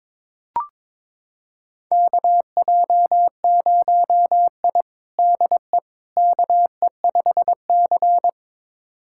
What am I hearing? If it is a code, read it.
Morse code, heard as a keyed tone, about 22 wpm, 700 Hz. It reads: KJ0I DE KE5C
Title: USA Callsign DE Callsign - 4 Characters - 22wpm